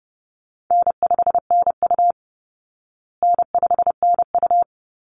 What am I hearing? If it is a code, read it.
Morse code, heard as a keyed tone, about 30 wpm, 700 Hz. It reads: N5NU N5NU
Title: Easy Call Signs Repeated 2x - 30wpm